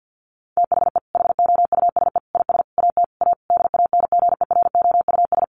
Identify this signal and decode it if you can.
Morse code, heard as a keyed tone, about 49 wpm, 700 Hz. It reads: THE HOUSE IS AT A DANGEROUS
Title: Sentences from Top 1500 Words 50wpm